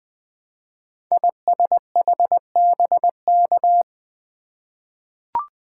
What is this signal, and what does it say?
Morse code, heard as a keyed tone, about 20 wpm, 700 Hz. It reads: ISHBK
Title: Code Groups - 5 Letters - 20wpm